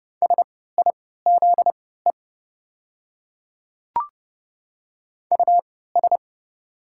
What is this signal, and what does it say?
Morse code, heard as a keyed tone, about 30 wpm, spaced as for 17 wpm, 700 Hz. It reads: SIZE US
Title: 3x Character Spacing - Top 500 Words - 30wpm